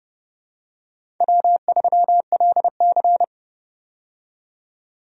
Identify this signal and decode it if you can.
Morse code, heard as a keyed tone, about 30 wpm, 700 Hz. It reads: W3LC
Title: USA Call Signs - 4 Characters - 30wpm